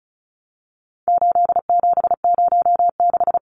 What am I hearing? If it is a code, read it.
Morse code, heard as a keyed tone, about 34 wpm, 700 Hz. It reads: 8706
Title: Numbers - 4-Digit 35wpm